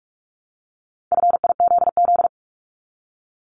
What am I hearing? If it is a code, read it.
Morse code, heard as a keyed tone, about 46 wpm, 700 Hz. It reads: FIZZ
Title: Multi-letter Phonemes Repeated 3x 45wpm